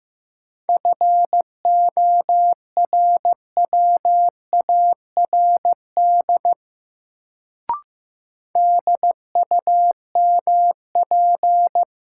Morse code, heard as a keyed tone, about 15 wpm, 700 Hz. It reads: FORWARD DUMP
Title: Sentences from Top 1300 Words - New Word Drill - 15wpm